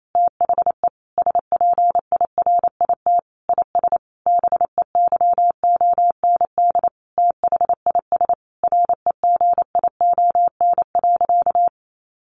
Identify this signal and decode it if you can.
Morse code, heard as a keyed tone, about 28 wpm, 700 Hz. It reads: THE SPIRIT IS BEYOND THIS REGION.
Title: Sentences from Top 1500 Words 28wpm